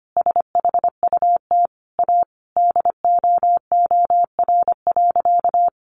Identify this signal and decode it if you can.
Morse code, heard as a keyed tone, about 25 wpm, 700 Hz. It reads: SHUT A DOOR.
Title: A Sound Of Thunder 25wpm